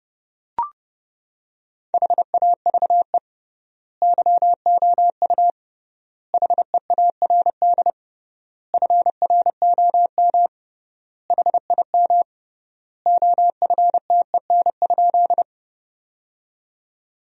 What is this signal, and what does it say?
Morse code, heard as a keyed tone, about 32 wpm, 700 Hz. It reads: HAVE YOU HEARD FROM HIM OFTEN?
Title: 3x Word-Spacing - Sentences from Top 400 Words - 30wpm